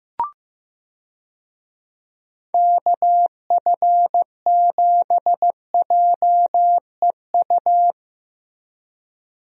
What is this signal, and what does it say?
Morse code, heard as a keyed tone, about 16 wpm, 700 Hz. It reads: KF7JEU
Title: USA Call Signs - 6 Characters - Repeated 3x - 15wpm